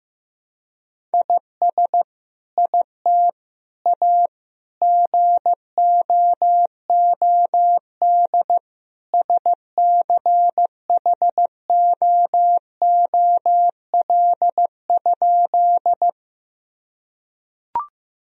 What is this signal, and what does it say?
Morse code, heard as a keyed tone, about 15 wpm, 700 Hz. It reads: IS IT A GOOD SCHOOL?
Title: Sentences from Top 1000 Words - Review Part 1 - 15wpm